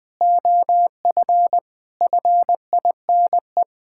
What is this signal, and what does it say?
Morse code, heard as a keyed tone, about 20 wpm, 700 Hz. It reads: OF FINE
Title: Sentences from Top 1600 Words 20wpm